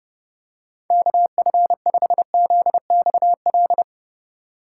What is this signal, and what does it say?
Morse code, heard as a keyed tone, about 30 wpm, 700 Hz. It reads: KF5ZXL
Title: USA Call Signs - 6 Characters - 2x Character Spacing - 30wpm